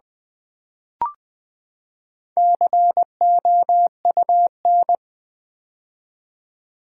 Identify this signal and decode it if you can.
Morse code, heard as a keyed tone, about 20 wpm, 700 Hz. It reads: COUN
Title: Build A Word - 4 Letters At A Time - 11-Letter Words - 20wpm